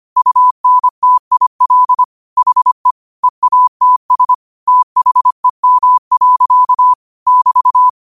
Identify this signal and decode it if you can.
Morse code, heard as a keyed tone, about 25 wpm, 1000 Hz. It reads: UNTIL HE EATS THEM. <BT>